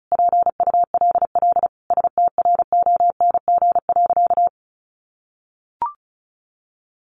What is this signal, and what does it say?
Morse code, heard as a keyed tone, about 34 wpm, 700 Hz. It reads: PULL STRONG.
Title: Sentences from Top 1500 Words 35wpm